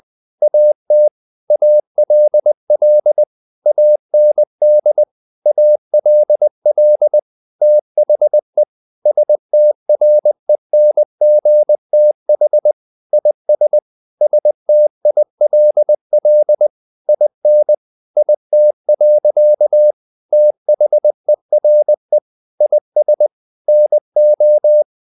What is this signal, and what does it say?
Morse code, heard as a keyed tone, about 20 wpm, 600 Hz. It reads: AT ALL AND ALL THE STRENGTH IS STILL IN IT. THERE IS NO
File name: the old man and the sea / the old man and the sea part 1 20wpm